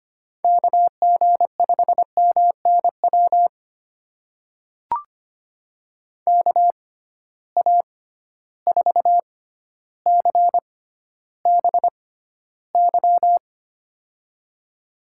Morse code, heard as a keyed tone, about 25 wpm, spaced as for 13 wpm, 700 Hz. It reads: KG5MNW KA4CBY
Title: USA Call Signs - 6 Characters - 6x Character Spacing - 25wpm